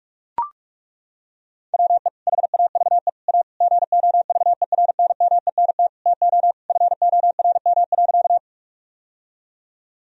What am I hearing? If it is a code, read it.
Morse code, heard as a keyed tone, about 42 wpm, 700 Hz. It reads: WE HAVE A GOVERNMENT TO FORM.
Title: Sentences from Top 1000 Words - Review Part 1 - 45wpm